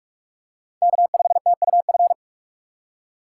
Morse code, heard as a keyed tone, about 43 wpm, 700 Hz. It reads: K5TUF